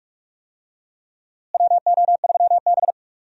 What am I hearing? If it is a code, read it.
Morse code, heard as a keyed tone, about 46 wpm, 700 Hz. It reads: WO3B
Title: USA Call Signs - 4 Characters - Repeated 3x - 45wpm